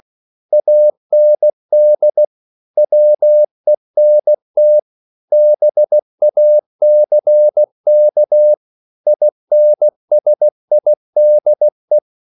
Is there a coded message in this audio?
Yes: Morse code from 0.5 to 12.0 s, heard as a keyed tone, about 16 wpm, 600 Hz. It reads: AND WENT BACK INSIDE